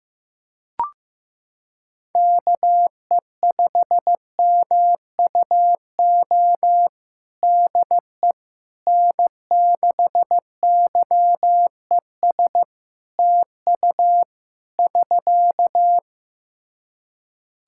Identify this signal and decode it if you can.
Morse code, heard as a keyed tone, about 15 wpm, 700 Hz. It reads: KE5MUO DE N6YES TU <SK>
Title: K1USN SST QSO Practice 15wpm